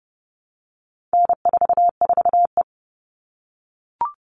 Morse code, heard as a keyed tone, about 30 wpm, 700 Hz. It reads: N44E
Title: All Cut Numbers - 4-Digit 30wpm